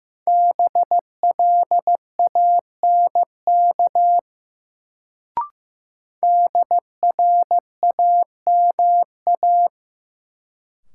Morse code, heard as a keyed tone, about 15 wpm, 700 Hz. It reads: BLANK DRAMA
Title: Sentences from Top 1500 Words - New Word Drill Alt - 15wpm